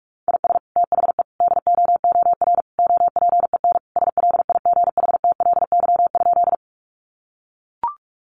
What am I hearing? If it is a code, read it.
Morse code, heard as a keyed tone, about 44 wpm, 700 Hz. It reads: IS THE DOOR OPEN SLIGHTLY?
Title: Sentences from Top 1300 Words 45wpm